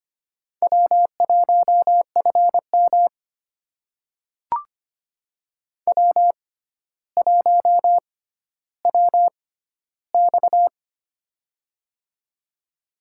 Morse code, heard as a keyed tone, about 25 wpm, spaced as for 13 wpm, 700 Hz. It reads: W1FM W1WX
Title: USA Call Signs - 6x Character Spacing - 25wpm